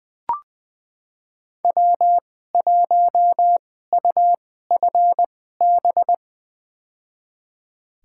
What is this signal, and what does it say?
Morse code, heard as a keyed tone, about 21 wpm, spaced as for 15 wpm, 700 Hz. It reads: W1UFB
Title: USA Call Signs - 5 Characters - 2x Character Spacing - 20wpm